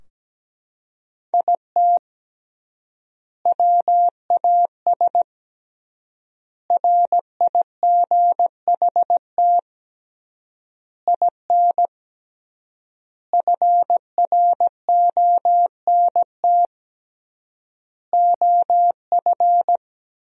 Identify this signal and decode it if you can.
Morse code, heard as a keyed tone, about 17 wpm, 700 Hz. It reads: IT WAS RIGHT IN FRONT OF
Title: 3x Word-Spacing - Sentences from Top 500 Words - 17wpm